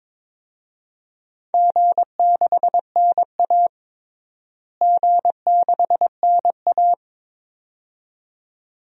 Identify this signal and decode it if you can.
Morse code, heard as a keyed tone, about 22 wpm, 700 Hz. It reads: G6NA G6NA
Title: DXCC-100 Call Signs - 4 Characters - Repeated 2x - 22wpm